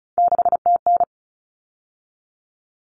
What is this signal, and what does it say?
Morse code, heard as a keyed tone, about 34 wpm, 700 Hz. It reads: 6TN
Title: Common Cut Numbers - 3-Digit 35wpm